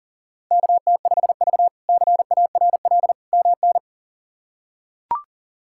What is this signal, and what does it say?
Morse code, heard as a keyed tone, about 40 wpm, 690 Hz. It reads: KT5V CARL MN
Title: K1USN SST QSO Practice - 40wpm